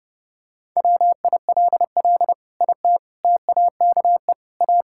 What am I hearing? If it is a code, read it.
Morse code, heard as a keyed tone, about 30 wpm, 700 Hz. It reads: WILL IT TAKE A
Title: Sentences from Top 1000 Words - Review Part 6 - 30wpm